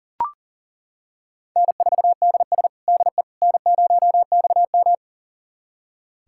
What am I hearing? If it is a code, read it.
Morse code, heard as a keyed tone, about 40 wpm, 700 Hz. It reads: N4DS DE N0XM
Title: USA Callsign DE Callsign - 4 Characters - 40wpm